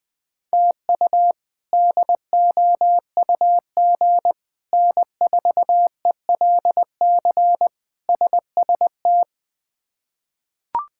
Morse code, heard as a keyed tone, about 21 wpm, 700 Hz. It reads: TU DOUG N4ELC SST
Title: K1USN SST QSO Practice - 20wpm